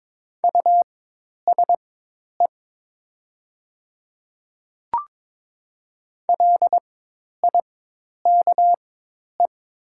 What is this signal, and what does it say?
Morse code, heard as a keyed tone, about 22 wpm, spaced as for 10 wpm, 700 Hz. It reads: USE LIKE
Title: 4x Character Spacing - Top 100 Words - 22wpm